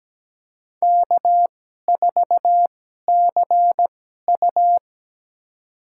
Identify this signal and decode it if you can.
Morse code, heard as a keyed tone, about 17 wpm, spaced as for 12 wpm, 700 Hz. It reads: K4CU
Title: USA Call Signs - 4 Characters - 2x Character Spacing - 17wpm